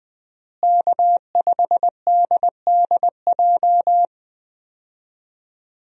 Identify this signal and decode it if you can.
Morse code, heard as a keyed tone, about 20 wpm, 700 Hz. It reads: K5DDJ
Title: USA Call Signs - 5 Characters - Repeated 2x - 20wpm